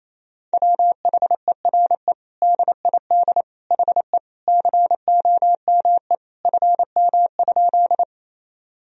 Morse code, heard as a keyed tone, about 28 wpm, 700 Hz. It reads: WHERE DID HE COME FM?
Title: Sentences from Top 200 Words and Abbreviations 28wpm